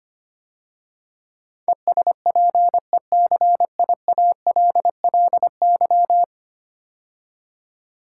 Morse code, heard as a keyed tone, about 25 wpm, 700 Hz. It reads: ESPECIALLY
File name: 10 Letter Words - First 3 Letters Given - 25wpm